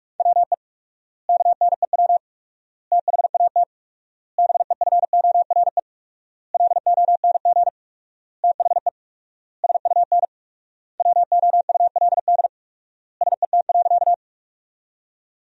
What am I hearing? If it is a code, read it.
Morse code, heard as a keyed tone, about 42 wpm, 700 Hz. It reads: WE KNEW THAT BEFORE LONG THE SUN WOULD SET.
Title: 4x Word-Spacing - Sentences from Top 400 Words - 45wpm